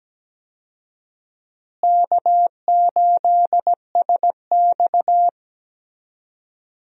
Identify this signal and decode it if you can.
Morse code, heard as a keyed tone, about 17 wpm, 700 Hz. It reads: K8SX
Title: USA Call Signs - 4 Characters - Repeated 3x - 17wpm